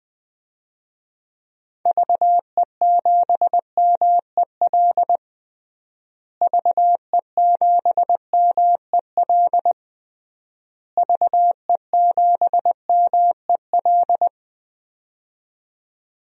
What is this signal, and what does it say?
Morse code, heard as a keyed tone, about 20 wpm, 700 Hz. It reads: VE7MEL VE7MEL VE7MEL
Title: DXCC-100 Call Signs - 6 Characters - Repeated 3x - 20wpm